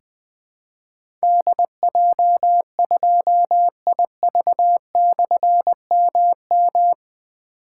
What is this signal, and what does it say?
Morse code, heard as a keyed tone, about 20 wpm, 700 Hz. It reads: DJ2IV/MM
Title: Difficult Call Signs 20wpm